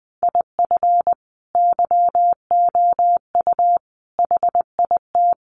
Morse code, heard as a keyed tone, about 20 wpm, 700 Hz. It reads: IF YOU HIT
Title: A Sound Of Thunder 20wpm